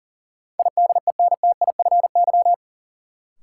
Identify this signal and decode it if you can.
Morse code, heard as a keyed tone, about 40 wpm, 700 Hz. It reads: IDENTIFY
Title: Sentences from Top 900 Words - New Word Drill - 40wpm